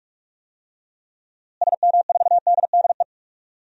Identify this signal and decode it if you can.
Morse code, heard as a keyed tone, about 46 wpm, 700 Hz. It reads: SM4DDE
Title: DXCC-100 Call Signs - 6 Characters - Repeated 2x - 45wpm